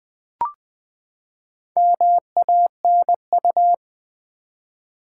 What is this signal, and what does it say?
Morse code, heard as a keyed tone, about 20 wpm, 700 Hz. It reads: MANU
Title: Build A Word - 4 Letters At A Time - 10-Letter Words - 20wpm